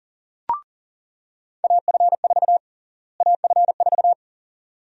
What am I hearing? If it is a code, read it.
Morse code, heard as a keyed tone, about 40 wpm, 700 Hz. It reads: AF4 AF4
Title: USA Call Sign Prefixes - Repeated 2x - 40wpm